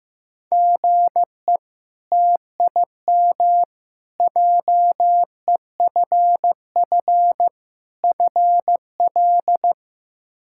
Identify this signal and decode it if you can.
Morse code, heard as a keyed tone, about 16 wpm, 700 Hz. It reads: GE TIM JEFF FL
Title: K1USN SST QSO Practice - 15wpm